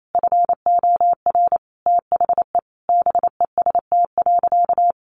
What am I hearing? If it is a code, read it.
Morse code, heard as a keyed tone, about 28 wpm, 700 Hz. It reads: FOR THE BEST.
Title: Sentences from Top 1000 Words 28wpm